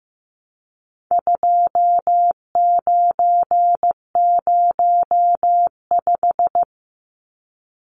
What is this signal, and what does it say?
Morse code, heard as a keyed tone, about 15 wpm, 700 Hz. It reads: 2905
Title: Numbers - 4-Digit 15wpm